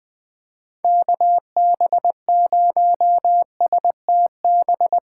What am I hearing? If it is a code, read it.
Morse code, heard as a keyed tone, about 21 wpm, 700 Hz. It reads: KB0STB
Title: USA Callsign DE Callsign - 6 Characters - 20wpm